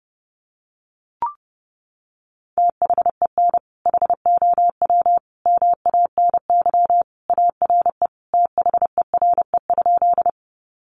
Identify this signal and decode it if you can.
Morse code, heard as a keyed tone, about 28 wpm, 700 Hz. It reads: THEN HOW MANY ARE THERE?
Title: Sentences from Top 100 Words 30wpm